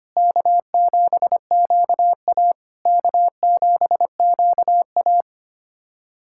Morse code, heard as a keyed tone, about 25 wpm, 690 Hz. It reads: K7QA K7QA
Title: SOTA QSO Practice 25wpm